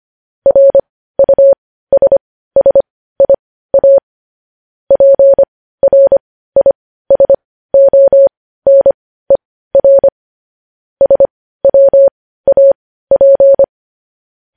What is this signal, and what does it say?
Morse code, heard as a keyed tone, about 25 wpm, spaced as for 15 wpm, 550 Hz. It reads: RUSSIA PRISONER SWAP
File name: podcast / News headlines-1x-25x15